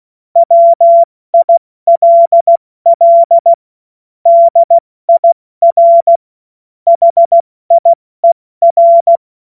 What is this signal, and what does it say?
Morse code, heard as a keyed tone, about 16 wpm, 680 Hz. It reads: WILL DIR HIER
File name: Morsekurs / Witze / Witz 28.November 2023 70 F80